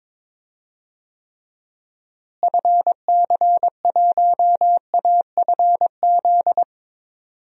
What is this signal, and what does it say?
Morse code, heard as a keyed tone, about 22 wpm, 700 Hz. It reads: FC1AFZ
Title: DXCC-100 Call Signs - 6 Characters - Repeated 2x - 22wpm